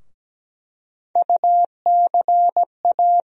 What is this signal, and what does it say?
Morse code, heard as a keyed tone, about 17 wpm, 700 Hz. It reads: UCA
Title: USA Call Sign Postfixes - 3 Letters 17wpm